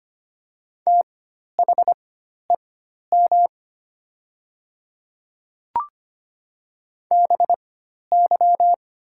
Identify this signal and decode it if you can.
Morse code, heard as a keyed tone, about 25 wpm, spaced as for 12 wpm, 700 Hz. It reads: THEM BY
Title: 4x Character Spacing - Top 100 Words - 25wpm